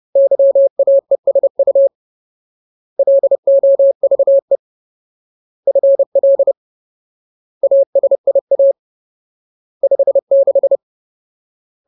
Morse code, heard as a keyed tone, about 30 wpm, 550 Hz. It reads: YAESU LOVE FL ASIA 56
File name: Ultimate Word List / 30WPM Ultimate